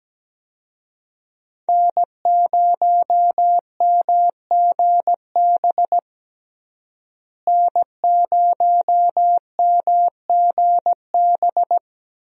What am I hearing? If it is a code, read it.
Morse code, heard as a keyed tone, about 17 wpm, 700 Hz. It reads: N0MGB N0MGB
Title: USA Call Signs - 5 Characters - Repeated 3x - 17wpm